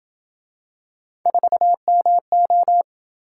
Morse code, heard as a keyed tone, about 26 wpm, 700 Hz. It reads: 4MO